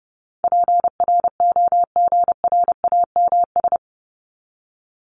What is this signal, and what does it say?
Morse code, heard as a keyed tone, about 30 wpm, 700 Hz. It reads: PROGRAMS
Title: ES:S Suffix - Top 100 Words with Base Words 30wpm